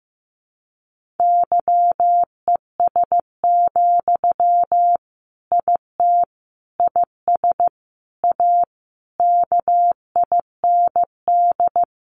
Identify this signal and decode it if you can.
Morse code, heard as a keyed tone, about 14 wpm, 700 Hz. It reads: YES, IT IS A KIND
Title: Sentences from Top 1400 Words 15wpm